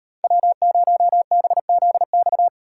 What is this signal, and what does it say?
Morse code, heard as a keyed tone, about 35 wpm, 700 Hz. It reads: W0BZX